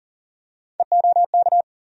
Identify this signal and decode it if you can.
Morse code, heard as a keyed tone, about 39 wpm, 700 Hz. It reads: EOK
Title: USA Call Sign Postfixes - 3 Letters 40wpm